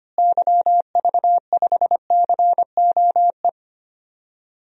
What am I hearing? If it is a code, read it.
Morse code, heard as a keyed tone, about 25 wpm, 700 Hz. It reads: YV5COE
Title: DXCC-100 Call Signs - 6 Characters - Repeated 3x - 25wpm